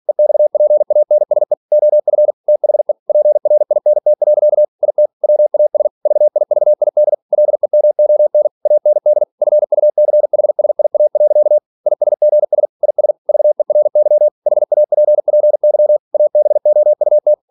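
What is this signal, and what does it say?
Morse code, heard as a keyed tone, about 46 wpm, 600 Hz. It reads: EXPANSE OF THE PRINT. IT WAS VIVID LEMON AND FUCHSIA. ISGS IS VERY HAPPY ABOUT
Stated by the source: Practice text from an English novel sent at 47 WPM.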